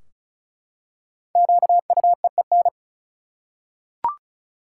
Morse code, heard as a keyed tone, about 36 wpm, 700 Hz. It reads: QUEEN
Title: Five Letter Words - 35wpm